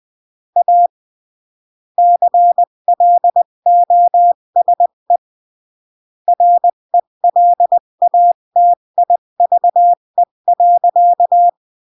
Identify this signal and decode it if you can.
Morse code, heard as a keyed tone, about 20 wpm, 700 Hz. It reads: A CLOSE RELATIVE.
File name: CW / jodorowsky / 82 commandments 20-15